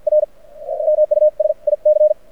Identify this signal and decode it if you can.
Morse code, heard as a keyed tone, about 52 wpm, spaced as for 42 wpm, 610 Hz. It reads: R MUSIC
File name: archived / CRSnet